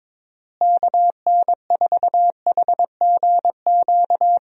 Morse code, heard as a keyed tone, about 22 wpm, 700 Hz. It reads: KN4HGQ
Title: USA Call Signs - 6 Characters - Repeated 3x - 22wpm